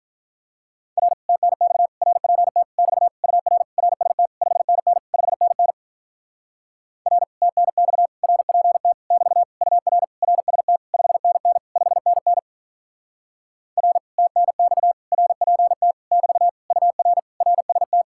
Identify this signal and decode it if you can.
Morse code, heard as a keyed tone, about 48 wpm, 700 Hz. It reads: R TNX RPT <BT> UR RST 5NN 5NN R TNX RPT <BT> UR RST 5NN 5NN R TNX RPT <BT> UR RST